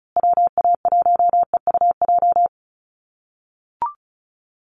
Morse code, heard as a keyed tone, about 34 wpm, 700 Hz. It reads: WA1EUJ
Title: USA Call Signs Repeated 3x 35wpm